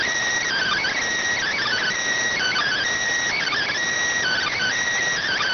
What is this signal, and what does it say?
Emergency Alert System (SAME) tones (header bursts), heard as harsh buzzing bursts. It is heard from 0.5 to 4.8 s.